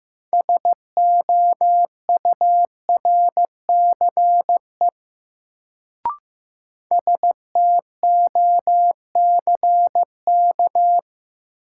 Morse code, heard as a keyed tone, about 15 wpm, 700 Hz. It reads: SOURCE STOCK
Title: Sentences from Top 800 Words - New Word Drill - 15wpm